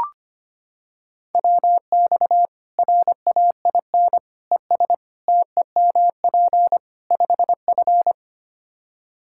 Morse code, heard as a keyed tone, about 25 wpm, 700 Hz. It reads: WX RAIN ES TEMP 5F
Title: K1USN SST QSO Practice 25wpm